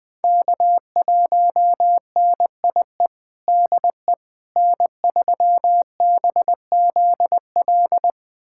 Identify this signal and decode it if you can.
Morse code, heard as a keyed tone, about 20 wpm, 700 Hz. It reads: K1NIE DE N3BZL
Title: USA Callsign DE Callsign - 5 Characters - 20wpm